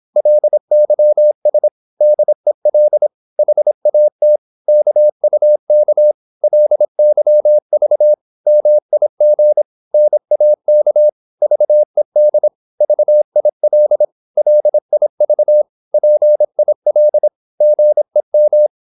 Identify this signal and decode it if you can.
Morse code, heard as a keyed tone, about 26 wpm, 600 Hz. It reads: LYS DEL HAT KUK LYV MIG NAK VED VIL LIV PIL GEM
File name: Korte ord DK 26wpm